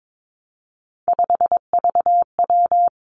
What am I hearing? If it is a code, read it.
Morse code, heard as a keyed tone, about 22 wpm, 700 Hz. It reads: 5VW
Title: Three Challenging Letter-Number Combinations 22wpm